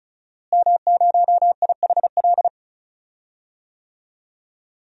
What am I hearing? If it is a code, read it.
Morse code, heard as a keyed tone, about 36 wpm, 700 Hz. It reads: M0IHL
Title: DXCC-100 Call Signs - 5 Characters - Repeated 3x - 35wpm